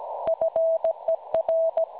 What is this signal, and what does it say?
Morse code, heard as a keyed tone, about 16 wpm, 650 Hz. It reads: FER
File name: cwronde / websdr recording .6kHz